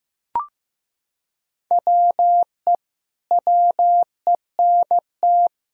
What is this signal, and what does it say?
Morse code, heard as a keyed tone, about 15 wpm, 700 Hz. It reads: WE WENT
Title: Sentences from Top 1000 Words - Review Part 3 - 15wpm